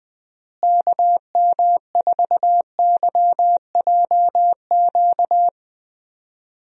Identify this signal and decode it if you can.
Morse code, heard as a keyed tone, about 20 wpm, 700 Hz. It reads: KM4YJQ
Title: USA Call Signs - 6 Characters - Repeated 3x - 20wpm